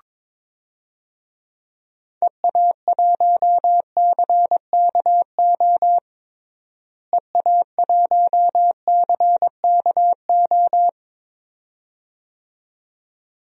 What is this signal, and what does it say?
Morse code, heard as a keyed tone, about 22 wpm, 700 Hz. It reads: EA1CKO EA1CKO
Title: DXCC-100 Call Signs - 6 Characters - Repeated 2x - 22wpm